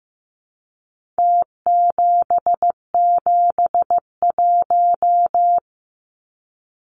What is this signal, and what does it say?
Morse code, heard as a keyed tone, about 15 wpm, 700 Hz. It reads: T771
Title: Numbers - Contest Style - 4-Digit 15wpm